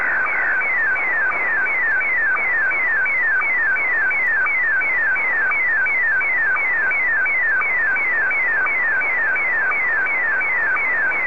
SSTV (slow-scan television), Scottie DX.